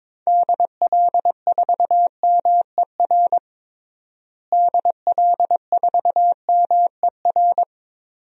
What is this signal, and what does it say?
Morse code, heard as a keyed tone, about 22 wpm, 700 Hz. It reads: DL4MER DL4MER
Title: DXCC-100 Call Signs - 6 Characters - Repeated 2x - 22wpm